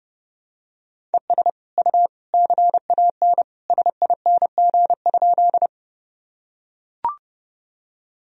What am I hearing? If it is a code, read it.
Morse code, heard as a keyed tone, about 30 wpm, 700 Hz. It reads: ES U CAN SING?
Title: Sentences from Top 400 Words and Abbreviations 30wpm